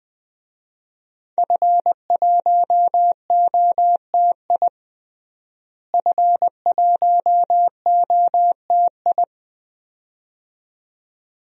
Morse code, heard as a keyed tone, about 20 wpm, 700 Hz. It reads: F1OTI F1OTI
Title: DXCC-100 Call Signs - 5 Characters - Repeated 2x - 20wpm